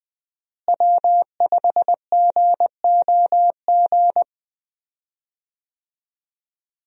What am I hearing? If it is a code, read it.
Morse code, heard as a keyed tone, about 20 wpm, 700 Hz. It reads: W5GOG
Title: USA Call Signs - 5 Characters - Repeated 2x - 20wpm